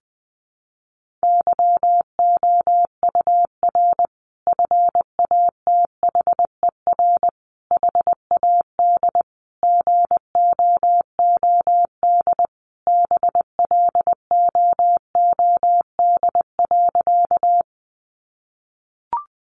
Morse code, heard as a keyed tone, about 19 wpm, 700 Hz. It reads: YOUR FATHER HAD GOOD BLOOD.
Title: Sentences from Top 1100 Words 20wpm